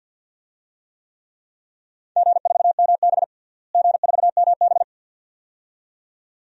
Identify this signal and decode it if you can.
Morse code, heard as a keyed tone, about 51 wpm, 700 Hz. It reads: G4MB G4MB
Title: DXCC-100 Call Signs - 4 Characters - Repeated 2x - 50wpm